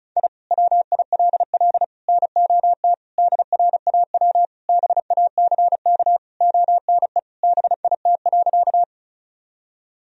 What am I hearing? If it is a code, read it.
Morse code, heard as a keyed tone, about 36 wpm, 700 Hz. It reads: I WILL NOT DRAW BACK ONE BIT.
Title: Sentences from Top 1000 Words - Review Part 1 - 35wpm